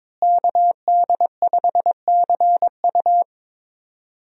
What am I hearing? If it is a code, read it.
Morse code, heard as a keyed tone, about 22 wpm, 700 Hz. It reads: KD5CU
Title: USA Call Signs - 5 Characters - Repeated 3x - 22wpm